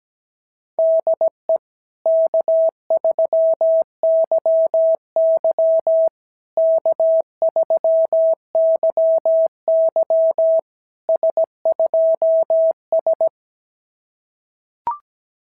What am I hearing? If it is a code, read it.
Morse code, heard as a keyed tone, about 17 wpm, 650 Hz. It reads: DE K3YY K3YY S2S
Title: SOTA QSO Practice 17wpm